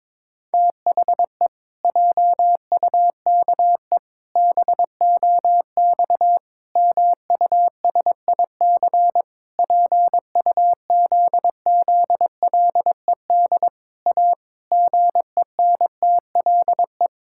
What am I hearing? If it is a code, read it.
Morse code, heard as a keyed tone, about 22 wpm, 700 Hz. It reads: THE JUKE BOX MUSIC PUZZLED A GENTLE
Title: Pangram Challenge v2 - 22wpm